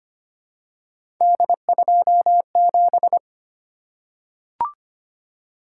Morse code, heard as a keyed tone, about 25 wpm, 700 Hz. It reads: D27
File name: Three Letter-Number Combinations - 25wpm